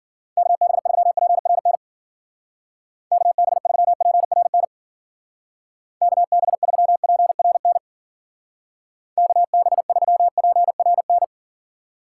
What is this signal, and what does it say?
Morse code, heard as a keyed tone, about 53 wpm, 700 Hz. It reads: KB3PRN KB3PRN KB3PRN KB3PRN